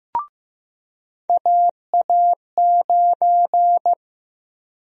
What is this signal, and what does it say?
Morse code, heard as a keyed tone, about 15 wpm, 700 Hz. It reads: AA9
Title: USA Call Sign Prefixes - 15wpm